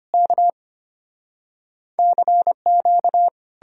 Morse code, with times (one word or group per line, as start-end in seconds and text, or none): 0.1-0.5 K
2.0-3.3 CQ